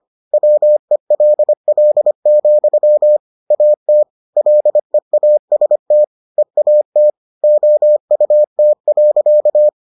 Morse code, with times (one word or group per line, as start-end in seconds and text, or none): 0.3-3.2 WELL,
3.5-4.0 AT
4.4-6.0 LEAST
6.4-7.1 EAT
7.4-9.7 OUT.